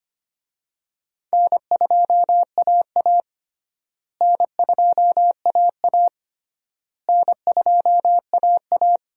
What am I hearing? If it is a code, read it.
Morse code, heard as a keyed tone, about 25 wpm, 700 Hz. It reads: N2AA N2AA N2AA
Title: USA Call Signs - 4 Characters - Repeated 3x - 25wpm